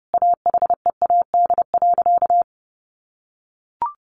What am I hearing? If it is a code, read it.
Morse code, heard as a keyed tone, about 28 wpm, 700 Hz. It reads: AHEAD.
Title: Sentences from Top 1100 Words 30wpm